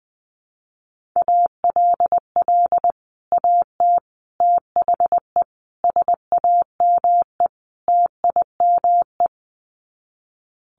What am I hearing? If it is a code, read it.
Morse code, heard as a keyed tone, about 20 wpm, 700 Hz. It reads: ALL AT THE SAME TIME
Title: Sets of 5 Words Encore2 20wpm